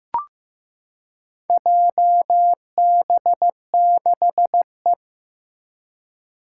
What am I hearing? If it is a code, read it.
Morse code, heard as a keyed tone, about 15 wpm, 700 Hz. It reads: JB6E
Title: Four Challenging Letter-Number Combinations 15wpm